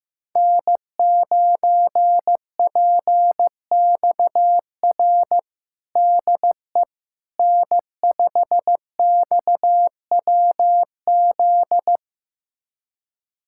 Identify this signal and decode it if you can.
Morse code, heard as a keyed tone, about 16 wpm, 700 Hz. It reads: N9PXR DE N5XWZ
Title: USA Callsign DE Callsign - 5 Characters - 15wpm